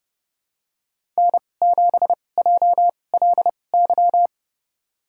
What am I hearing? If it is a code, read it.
Morse code, heard as a keyed tone, about 30 wpm, spaced as for 22 wpm, 700 Hz. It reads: N7JLY
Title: USA Call Signs - 5 Characters - 2x Character Spacing - 30wpm